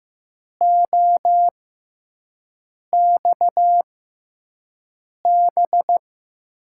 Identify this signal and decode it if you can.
Morse code, heard as a keyed tone, about 15 wpm, spaced as for 5 wpm, 700 Hz. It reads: OXB
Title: USA Call Signs - 5 Characters - 6x Character Spacing - 15wpm